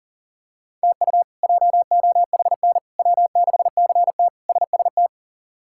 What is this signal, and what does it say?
Morse code, heard as a keyed tone, about 40 wpm, 700 Hz. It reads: TU JOHN W6CT SST
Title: K1USN SST QSO Practice - 40wpm